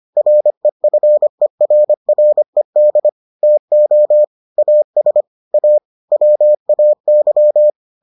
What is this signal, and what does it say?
Morse code, heard as a keyed tone, about 25 wpm, 600 Hz. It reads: REFERRED TO AS A WAY